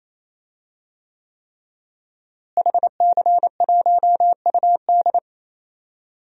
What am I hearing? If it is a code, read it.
Morse code, heard as a keyed tone, about 28 wpm, 700 Hz. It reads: HC1UD
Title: DXCC-100 Call Signs - 5 Characters - Repeated 3x - 28wpm